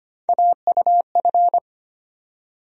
Morse code, heard as a keyed tone, about 25 wpm, 700 Hz. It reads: AUF
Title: USA Call Sign Postfixes - 3 Letters 25wpm